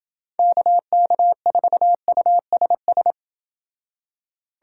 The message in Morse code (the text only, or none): KK4USS